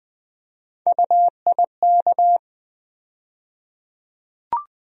Morse code, heard as a keyed tone, about 20 wpm, 700 Hz. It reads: UIK
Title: USA Call Sign Postfixes - 3 Letters 20wpm